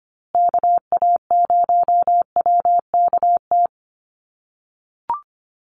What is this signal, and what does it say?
Morse code, heard as a keyed tone, about 25 wpm, 700 Hz. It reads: KA0WKT
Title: USA Call Signs Repeated 2x 25wpm